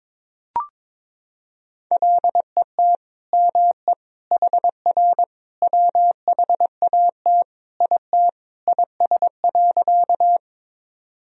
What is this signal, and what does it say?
Morse code, heard as a keyed tone, about 22 wpm, 700 Hz. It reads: LET ME HR WHAT IT IS.
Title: Sentences from Top 300 Words and Abbreviations 22wpm